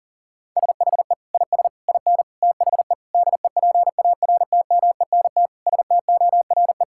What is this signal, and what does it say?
Morse code, heard as a keyed tone, about 40 wpm, 700 Hz. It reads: SHE IS IN THE DEPARTMENT STORE
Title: Sentences from Top 1000 Words - Review Part 5 - 40wpm